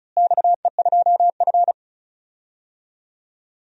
Morse code, heard as a keyed tone, about 36 wpm, 700 Hz. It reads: XE2F
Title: DXCC-100 Call Signs - 4 Characters - Repeated 3x - 35wpm